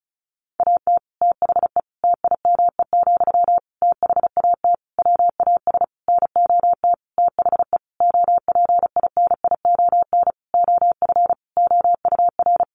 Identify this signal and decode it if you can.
Morse code, heard as a keyed tone, about 34 wpm, 700 Hz. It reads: AT THE TIME, THAT WAS NOT THE OPINION OF OUR
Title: Sentences from Top 1400 Words 35wpm